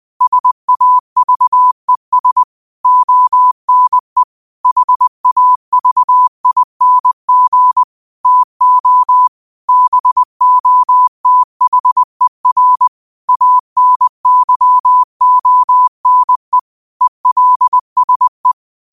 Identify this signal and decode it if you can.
Morse code, heard as a keyed tone, about 20 wpm, 1000 Hz. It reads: SAVES ONE HAVING TO BOTHER ANYONE ELSE